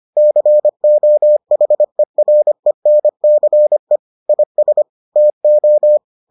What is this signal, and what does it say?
Morse code, heard as a keyed tone, about 25 wpm, 600 Hz. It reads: COHERENCE IS TO